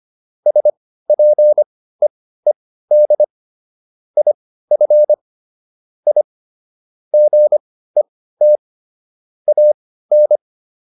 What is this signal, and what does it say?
Morse code, heard as a keyed tone, about 25 wpm, spaced as for 15 wpm, 600 Hz. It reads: SPEED IF I GET AN